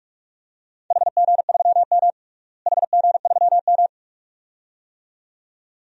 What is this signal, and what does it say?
Morse code, heard as a keyed tone, about 45 wpm, 700 Hz. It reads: HG3M HG3M
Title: DXCC-100 Call Signs - 4 Characters - Repeated 2x - 45wpm